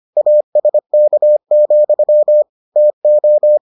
Morse code, heard as a keyed tone, about 25 wpm, 600 Hz. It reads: ASK, TO